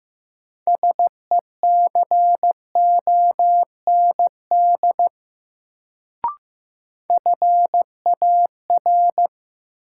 Morse code, heard as a keyed tone, about 15 wpm, 700 Hz. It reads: SECOND FAR
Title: Sentences from Top 300 Words - New Word Drill - 15wpm